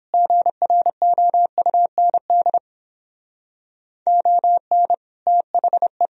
Morse code, heard as a keyed tone, about 30 wpm, 700 Hz. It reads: GROUND ON THE